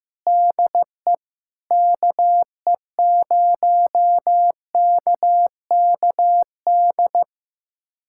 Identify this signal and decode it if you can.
Morse code, heard as a keyed tone, about 15 wpm, 700 Hz. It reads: DE KE0KKD
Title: USA Callsign DE Callsign - 6 Characters - 15wpm